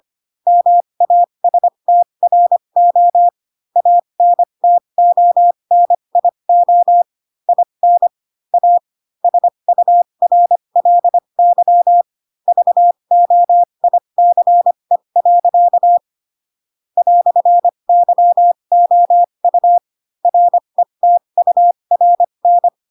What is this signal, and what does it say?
Morse code, heard as a keyed tone, about 25 wpm, spaced as for 22 wpm, 700 Hz. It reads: MASTRO ANTONIO IN A SURLY VOICE. "YOU RETURN
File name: Pinocchio / Pinocchio ch2 18 25 ik1hge